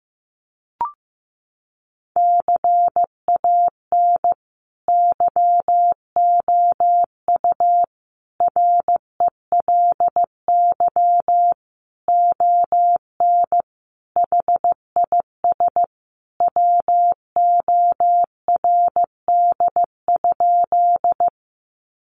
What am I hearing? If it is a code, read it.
Morse code, heard as a keyed tone, about 15 wpm, 700 Hz. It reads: CAN YOU RELY ON HIS WORD?
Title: Sentences from Top 1200 Words 15wpm